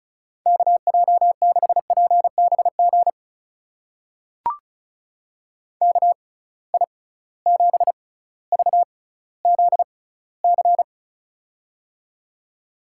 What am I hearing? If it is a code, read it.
Morse code, heard as a keyed tone, about 36 wpm, spaced as for 18 wpm, 700 Hz. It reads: KJ6PBG KI7VZC
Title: USA Call Signs - 6 Characters - 6x Character Spacing - 35wpm